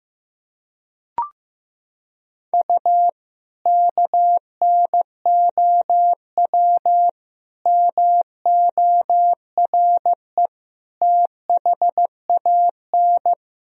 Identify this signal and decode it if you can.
Morse code, heard as a keyed tone, about 16 wpm, 700 Hz. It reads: U KNOW MORE THAN
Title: Sentences from Top 100 Words and Abbreviations 15wpm